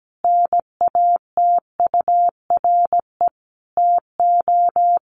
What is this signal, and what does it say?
Morse code, heard as a keyed tone, about 17 wpm, 700 Hz. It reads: NATURE TO
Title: Sentences from Top 1300 Words 17wpm